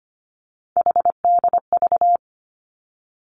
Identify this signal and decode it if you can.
Morse code, heard as a keyed tone, about 25 wpm, 700 Hz. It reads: HDV
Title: Three Challenging Letter-Number Combinations 25wpm